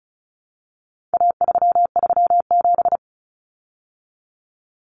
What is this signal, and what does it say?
Morse code, heard as a keyed tone, about 34 wpm, 700 Hz. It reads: A337
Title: Common Cut Numbers - 4-Digit 35wpm